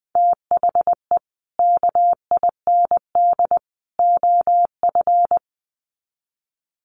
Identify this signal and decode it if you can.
Morse code, heard as a keyed tone, about 19 wpm, 700 Hz. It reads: THE KIND OF
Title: Sets of 3 Words 20wpm